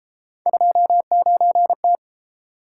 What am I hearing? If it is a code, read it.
Morse code, heard as a keyed tone, about 31 wpm, 700 Hz. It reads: 29T